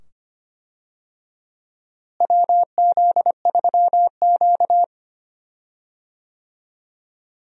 Morse code, heard as a keyed tone, about 25 wpm, 700 Hz. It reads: WZ3Q
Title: USA Call Signs - 4 Characters - Repeated 2x - 25wpm